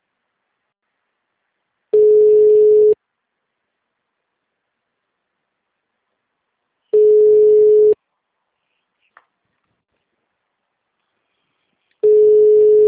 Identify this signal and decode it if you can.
Ringback tone (Europe, 425 Hz).
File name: out- -2005- - - .850